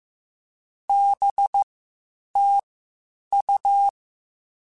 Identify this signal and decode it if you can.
Morse code, heard as a keyed tone, about 15 wpm, spaced as for 8 wpm, 800 Hz. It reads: BTU
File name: QSOs / QSOs-mp3s / WD0BGP 1@10